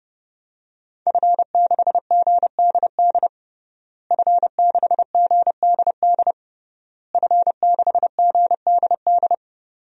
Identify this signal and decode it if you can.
Morse code, heard as a keyed tone, about 29 wpm, 700 Hz. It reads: F6GDD F6GDD F6GDD
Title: DXCC-100 Call Signs - 5 Characters - Repeated 3x - 30wpm